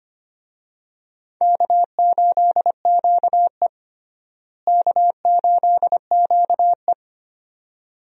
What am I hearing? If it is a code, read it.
Morse code, heard as a keyed tone, about 25 wpm, 700 Hz. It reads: K8QE K8QE
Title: USA Call Signs - 4 Characters - Repeated 2x - 25wpm